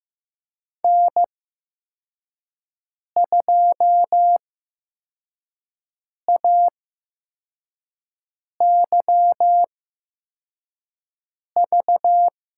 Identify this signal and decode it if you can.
Morse code, heard as a keyed tone, about 15 wpm, spaced as for 4 wpm, 700 Hz. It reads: N2AYV
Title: USA Call Signs - 5 Characters - 8x Character Spacing - 15wpm